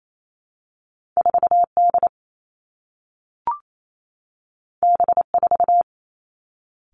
Morse code, heard as a keyed tone, about 28 wpm, 700 Hz. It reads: 4 D B 4
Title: Two Challenging Letter-Number Combinations 28wpm